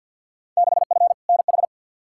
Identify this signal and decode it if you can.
Morse code, heard as a keyed tone, about 52 wpm, 700 Hz. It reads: 6F NH
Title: Field Day Exchanges Encore - 50wpm